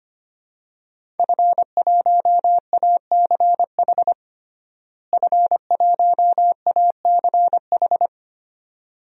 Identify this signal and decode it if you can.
Morse code, heard as a keyed tone, about 25 wpm, 700 Hz. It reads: F1ACH F1ACH
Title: DXCC-100 Call Signs - 5 Characters - Repeated 2x - 25wpm